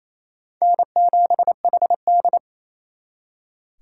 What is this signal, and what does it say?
Morse code, heard as a keyed tone, about 34 wpm, 700 Hz. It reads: N7HD
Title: Speed Racing - USA 1x2 Call Signs 28wpm